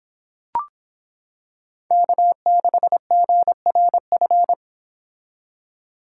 Morse code, heard as a keyed tone, about 26 wpm, 700 Hz. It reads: K6GRF